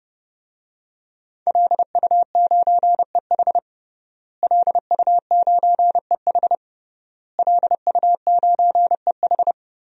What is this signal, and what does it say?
Morse code, heard as a keyed tone, about 30 wpm, 700 Hz. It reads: LU9EH LU9EH LU9EH
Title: DXCC-100 Call Signs - 5 Characters - Repeated 3x - 30wpm